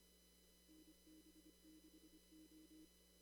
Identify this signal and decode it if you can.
Morse code, heard as a keyed tone, about 25 wpm, 310 Hz. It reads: NB6O
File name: CW / kt5j wpx cw-op-k5pi